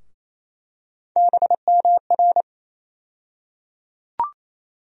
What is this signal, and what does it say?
Morse code, heard as a keyed tone, about 28 wpm, 700 Hz. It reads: BMR
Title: USA Call Sign Postfixes - 3 Letters 28wpm